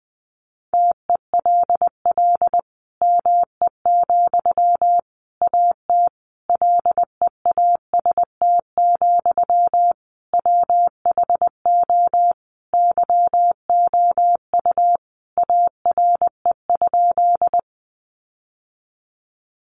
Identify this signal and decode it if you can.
Morse code, heard as a keyed tone, about 20 wpm, 700 Hz. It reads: TELL ME, AT LEAST, WHO YOU ARE?
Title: Sentences from Top 700 Words 20wpm